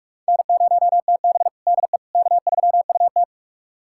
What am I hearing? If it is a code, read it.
Morse code, heard as a keyed tone, about 45 wpm, 700 Hz. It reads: N0TB DE K3UT
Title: USA Callsign DE Callsign - 4 Characters - 45wpm